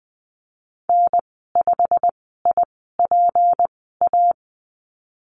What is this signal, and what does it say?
Morse code, heard as a keyed tone, about 20 wpm, spaced as for 15 wpm, 700 Hz. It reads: N5IPA
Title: 179 Lesson 29g - Slash - Call Signs 20wpm